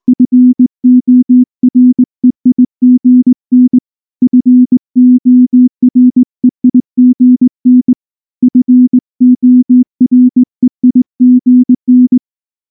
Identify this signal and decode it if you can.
Morse code, heard as a keyed tone, about 19 wpm, 250 Hz. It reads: FOREIGN FOREIGN FOREIGN
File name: morse / morse-training-20WPM-8